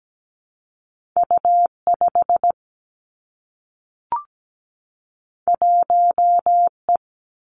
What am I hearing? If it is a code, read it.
Morse code, heard as a keyed tone, about 17 wpm, 700 Hz. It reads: U 5 1 E
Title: Two Challenging Letter-Number Combinations 17wpm